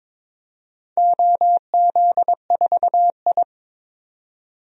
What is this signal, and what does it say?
Morse code, heard as a keyed tone, about 22 wpm, 700 Hz. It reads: OZ4I
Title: DXCC-100 Call Signs - 4 Characters - 22wpm